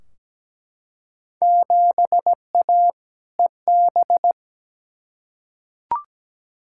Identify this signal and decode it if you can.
Morse code, heard as a keyed tone, about 17 wpm, 700 Hz. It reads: 7A EB
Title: Field Day Exchanges Encore - 17wpm